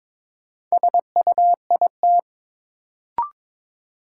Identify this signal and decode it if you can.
Morse code, heard as a keyed tone, about 23 wpm, 700 Hz. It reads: SUIT
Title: Sentences from Top 1600 Words - New Word Drill - 22wpm